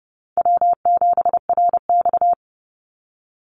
Morse code, heard as a keyed tone, about 30 wpm, 700 Hz. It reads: W7RX
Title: Call Signs - 30wpm